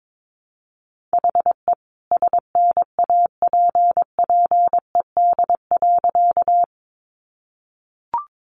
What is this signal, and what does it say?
Morse code, heard as a keyed tone, about 22 wpm, 700 Hz. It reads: HE SNAPPED.
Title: A Sound Of Thunder 22wpm